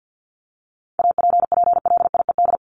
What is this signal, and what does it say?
Morse code, heard as a keyed tone, about 50 wpm, 700 Hz. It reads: APPLIED
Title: ED Suffix - Top 100 Words with Base Words 50wpm